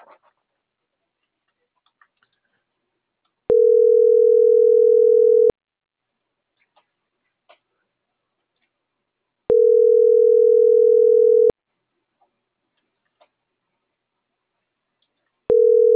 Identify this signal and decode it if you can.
Ringback tone (North America), 440 + 480 Hz.
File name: CENTRAL MABE